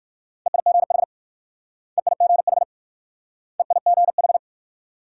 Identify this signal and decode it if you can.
Morse code, heard as a keyed tone, about 56 wpm, 700 Hz. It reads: EIGH EIGH EIGH
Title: Speed-Racing Multi-letter Phonemes - 45wpm